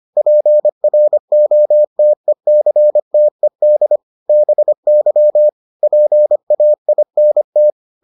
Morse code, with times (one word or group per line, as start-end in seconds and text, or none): 0.2-4.0 PROTECTED
4.3-5.5 BY
5.8-7.7 PAINT